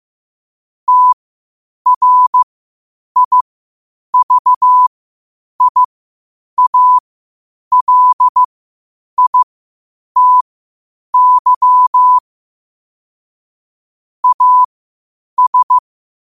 Morse code, heard as a keyed tone, about 15 wpm, spaced as for 8 wpm, 1000 Hz. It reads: TRIVIALITY AS
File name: qotdfiles / 8wpm